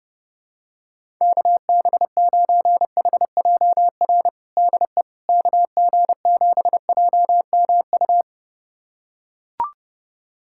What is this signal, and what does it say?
Morse code, heard as a keyed tone, about 32 wpm, 700 Hz. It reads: KB9HJR DE KG7JMU
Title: USA Callsign DE Callsign - 6 Characters - 30wpm